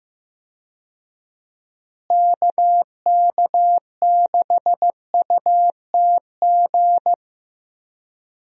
Morse code, heard as a keyed tone, about 16 wpm, 700 Hz. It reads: KK6UTG
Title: USA Call Signs - 6 Characters - Repeated 3x - 15wpm